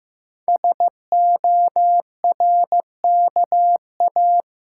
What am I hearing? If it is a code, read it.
Morse code, heard as a keyed tone, about 15 wpm, 700 Hz. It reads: SORKA
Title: Code Groups - 5 Letters - 15wpm